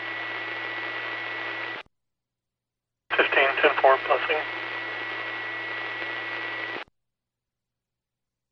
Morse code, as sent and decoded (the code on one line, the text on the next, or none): none